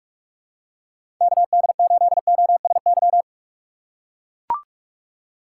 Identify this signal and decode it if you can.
Morse code, heard as a keyed tone, about 44 wpm, 700 Hz. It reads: K D 8 O S Y
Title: USA Call Signs - 6 Characters - 6x Character Spacing - 45wpm